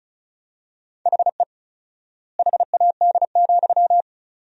Morse code, heard as a keyed tone, about 34 wpm, 700 Hz. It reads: HE HAD,
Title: 4x Word-Spacing - Sentences from Top 500 Words - 35wpm